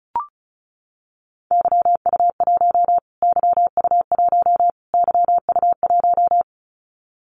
Morse code, heard as a keyed tone, about 34 wpm, 700 Hz. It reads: YU1 YU1 YU1
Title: Callsign Prefixes for Top 25 DXCC - Repeated 3x - 35wpm